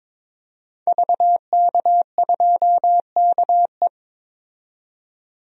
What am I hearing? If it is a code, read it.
Morse code, heard as a keyed tone, about 22 wpm, 700 Hz. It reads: VK2KE
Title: DXCC-100 Call Signs - 5 Characters - 22wpm